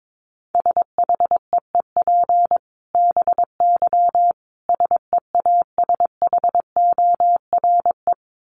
Morse code, heard as a keyed tone, about 22 wpm, 700 Hz. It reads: SHEEP BY SEASHORE
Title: Dit Challenge 22wpm